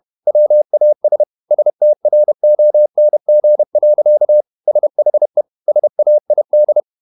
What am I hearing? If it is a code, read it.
Morse code, heard as a keyed tone, about 31 wpm, 600 Hz. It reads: WAS STRONG. SHE SAID
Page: Never 31wpm | CW med Gnister